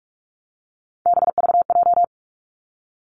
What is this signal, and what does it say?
Morse code, heard as a keyed tone, about 46 wpm, 700 Hz. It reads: BVJ
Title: Three Challenging Letter-Number Combinations 45wpm